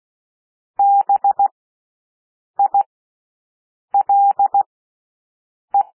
Morse code, heard as a keyed tone, about 16 wpm, spaced as for 6 wpm, 800 Hz. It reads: BILE
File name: cwmp3 / W1 SOUND WORDS 16 wpm